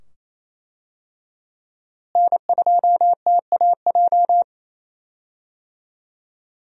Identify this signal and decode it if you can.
Morse code, heard as a keyed tone, about 28 wpm, 700 Hz. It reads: N2TAJ
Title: USA Call Signs - 5 Characters - Repeated 2x - 28wpm